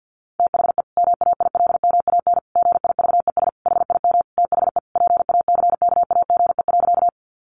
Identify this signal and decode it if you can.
Morse code, heard as a keyed tone, about 50 wpm, 700 Hz. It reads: THE MAILMAN GIVES HIM THE PACKAGE.
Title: Easy Phrases 50wpm